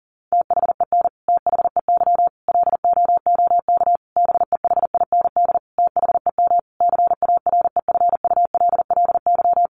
Morse code, heard as a keyed tone, about 40 wpm, 700 Hz. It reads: THEN THEY LOOK BEHIND THEM CAREFULLY
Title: Sentences from Top 1300 Words 40wpm